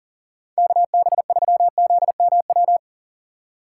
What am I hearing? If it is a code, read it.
Morse code, heard as a keyed tone, about 38 wpm, 700 Hz. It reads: KB3ZMW
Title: USA Call Signs - 6 Characters - 6x Character Spacing - 40wpm